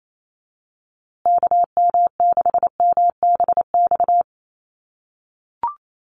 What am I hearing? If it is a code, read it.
Morse code, heard as a keyed tone, about 28 wpm, 700 Hz. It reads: KM6MBX
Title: USA Call Signs Repeated 3x 28wpm